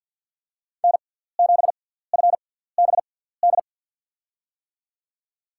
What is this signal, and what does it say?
Morse code, heard as a keyed tone, about 53 wpm, spaced as for 17 wpm, 700 Hz. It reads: N7FBD
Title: USA Call Signs - 5 Characters - 6x Character Spacing - 50wpm